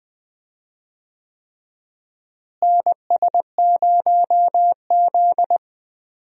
Morse code, heard as a keyed tone, about 20 wpm, 700 Hz. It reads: NS0Z
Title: USA Call Signs - 4 Characters - Repeated 3x - 20wpm